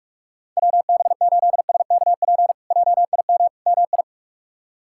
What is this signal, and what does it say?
Morse code, heard as a keyed tone, about 46 wpm, 700 Hz. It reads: WB8SKP JIM MI
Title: K1USN SST QSO Practice - 45wpm